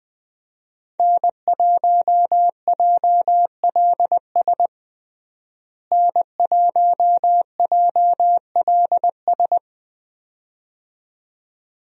Morse code, heard as a keyed tone, about 20 wpm, 700 Hz. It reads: N1JLS N1JLS
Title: USA Call Signs - 5 Characters - Repeated 2x - 20wpm